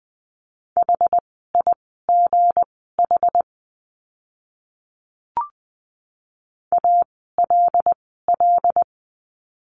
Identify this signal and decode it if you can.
Morse code, heard as a keyed tone, about 20 wpm, spaced as for 14 wpm, 700 Hz. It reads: HIGH ALL
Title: 160 Lesson 26f - Letter g - Words 20wpm